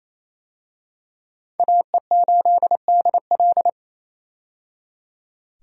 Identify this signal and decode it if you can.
Morse code, heard as a keyed tone, about 28 wpm, 700 Hz. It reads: AE8DL
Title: USA Call Signs - 5 Characters - 28wpm